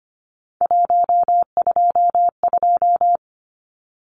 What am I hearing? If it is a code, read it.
Morse code, heard as a keyed tone, about 25 wpm, 700 Hz. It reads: 122
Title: Numbers - 3-Digit 25wpm